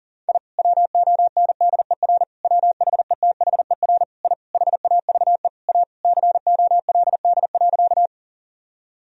Morse code, heard as a keyed tone, about 40 wpm, 700 Hz. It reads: I WONDER WHETHER I HAVE A COLD.
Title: Sentences from Top 1000 Words - Review Part 3 - 40wpm